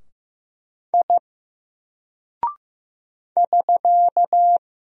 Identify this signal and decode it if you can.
Morse code, heard as a keyed tone, about 15 wpm, 700 Hz. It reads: I <SK>
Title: Single Character (Letter:Number:Punctuation:Prosign) Alt 15wpm